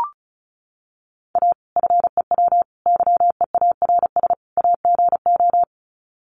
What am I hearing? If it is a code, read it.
Morse code, heard as a keyed tone, about 34 wpm, 700 Hz. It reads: A FEW YEARS AGO
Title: Sets of 4 Words 35wpm